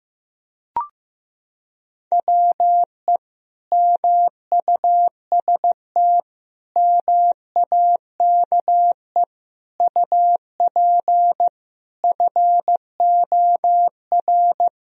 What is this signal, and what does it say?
Morse code, heard as a keyed tone, about 15 wpm, 700 Hz. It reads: WE MUST MAKE UP FOR
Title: Sentences from Top 1000 Words - Review Part 3 - 15wpm